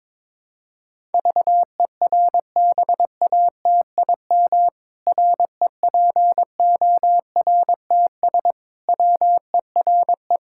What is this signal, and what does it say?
Morse code, heard as a keyed tone, about 22 wpm, 700 Hz. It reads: VERBATIM REPORTS WERE
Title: Pangram Challenge v2 - 22wpm